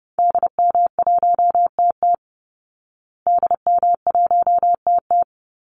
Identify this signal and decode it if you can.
Morse code, heard as a keyed tone, about 30 wpm, 700 Hz. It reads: DM1TT DM1TT
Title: Call Signs Repeated 3x - 30wpm